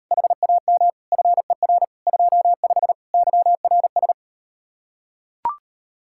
Morse code, heard as a keyed tone, about 39 wpm, 700 Hz. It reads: HAM FER 25 YRS